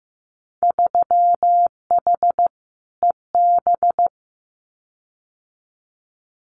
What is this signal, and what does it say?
Morse code, heard as a keyed tone, about 15 wpm, 700 Hz. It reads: 3H EB
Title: Winter Field Day Exchanges Encore 15wpm